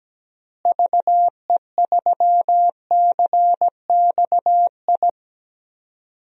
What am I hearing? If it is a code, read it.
Morse code, heard as a keyed tone, about 17 wpm, 700 Hz. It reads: VE3CXI
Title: DXCC-100 Call Signs - 6 Characters - Repeated 3x - 17wpm